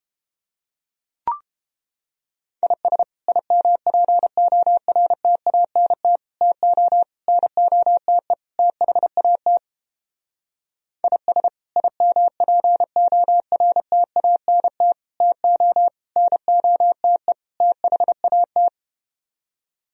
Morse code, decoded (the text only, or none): IS IMPORTANT TO NOTE THAT IS IMPORTANT TO NOTE THAT